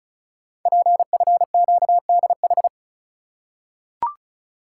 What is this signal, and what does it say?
Morse code, heard as a keyed tone, about 36 wpm, 700 Hz. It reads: PFQDH
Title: Code Groups - 5 Letters - 35wpm